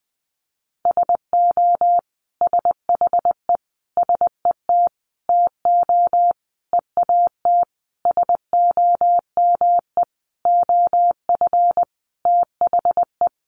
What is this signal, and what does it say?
Morse code, heard as a keyed tone, about 20 wpm, 700 Hz. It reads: SO SHE SET TO EAT SOME OF THE
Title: Sentences from Top 900 Words 20wpm